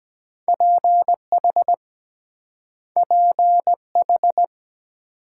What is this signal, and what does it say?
Morse code, heard as a keyed tone, about 23 wpm, 700 Hz. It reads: PH PH